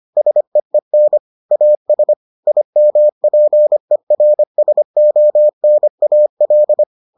Morse code, heard as a keyed tone, about 25 wpm, 600 Hz. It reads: SEEN AS IMPERSONAL